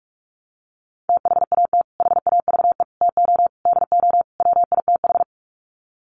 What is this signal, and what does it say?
Morse code, heard as a keyed tone, about 44 wpm, 700 Hz. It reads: THAT HAVE TO DO WITH
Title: Sets of 5 Words Encore2 45wpm